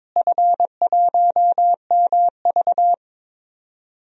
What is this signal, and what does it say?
Morse code, heard as a keyed tone, about 22 wpm, 700 Hz. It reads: F1MV
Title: DXCC-100 Call Signs - 4 Characters - Repeated 2x - 22wpm